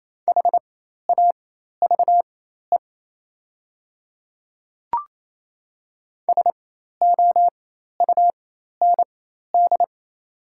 Morse code, heard as a keyed tone, about 28 wpm, spaced as for 13 wpm, 690 Hz. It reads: HAVE SOUND
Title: 4x Character Spacing - Top 200 Words - 28wpm